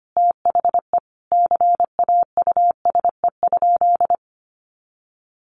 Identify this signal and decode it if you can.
Morse code, heard as a keyed tone, about 25 wpm, 700 Hz. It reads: THE CAUSE?
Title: Sentences from Top 200 Words 25wpm